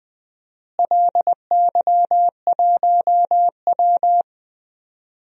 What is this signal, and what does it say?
Morse code, heard as a keyed tone, about 20 wpm, 700 Hz. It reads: LY1W
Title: DXCC-100 Call Signs - 4 Characters - Repeated 3x - 20wpm